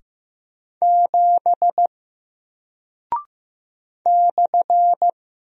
Morse code, heard as a keyed tone, about 15 wpm, 700 Hz. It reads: 7 /
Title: Single Character (Letter:Number:Punctuation:Prosign) Alt 15wpm